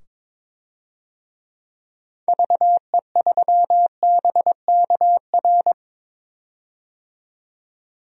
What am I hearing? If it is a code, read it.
Morse code, heard as a keyed tone, about 22 wpm, 700 Hz. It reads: VE3BKR
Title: DXCC-100 Call Signs - 6 Characters - 22wpm